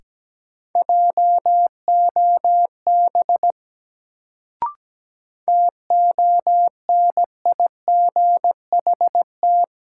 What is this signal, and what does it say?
Morse code, heard as a keyed tone, about 17 wpm, 700 Hz. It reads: JOB TONIGHT
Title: Sentences from Top 600 Words - New Word Drill - 17wpm